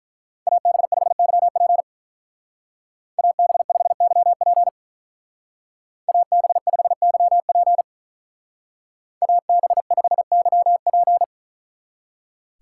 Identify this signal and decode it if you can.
Morse code, heard as a keyed tone, about 44 wpm, 700 Hz. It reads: AB5YP AB5YP AB5YP AB5YP